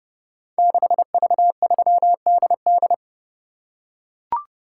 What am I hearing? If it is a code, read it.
Morse code, heard as a keyed tone, about 28 wpm, 700 Hz. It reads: 6V3DD
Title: Code Groups - 5 Letter:Numbers - 30wpm